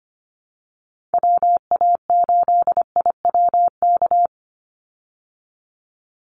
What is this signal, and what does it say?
Morse code, heard as a keyed tone, about 25 wpm, 700 Hz. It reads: WA8IWK
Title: Call Signs Repeated 2x - 25wpm